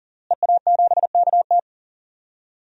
Morse code, heard as a keyed tone, about 40 wpm, 700 Hz. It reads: EA7KT
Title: DXCC-100 Call Signs - 5 Characters - Repeated 3x - 40wpm